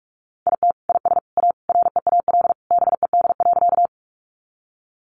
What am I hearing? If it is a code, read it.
Morse code, heard as a keyed tone, about 44 wpm, 700 Hz. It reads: IT IS A REAL BED.
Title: Sentences from Top 500 Words 45wpm